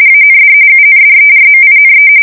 RTTY, about 45.45 baud. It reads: AND UNDERMINE
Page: 170 Shift 2125/2295